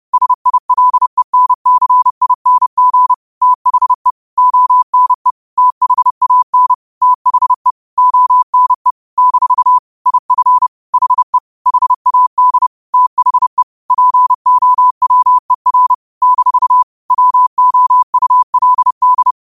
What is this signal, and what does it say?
Morse code, heard as a keyed tone, about 30 wpm, 1000 Hz. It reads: SILENCING THE ONE THAN THE ONE - IF HE HAD THE POWER - WOULD
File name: qotdfiles / 30wpm